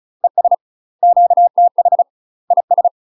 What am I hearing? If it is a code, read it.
Morse code, heard as a keyed tone, about 34 wpm, 700 Hz. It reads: ES QTH IS
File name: UK MASTER QSOs TRAINING / 35WPM / UK MASTER QSOs 0001